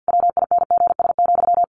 Morse code, heard as a keyed tone, about 48 wpm, 700 Hz. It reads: WINGS,
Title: A Sound Of Thunder 50wpm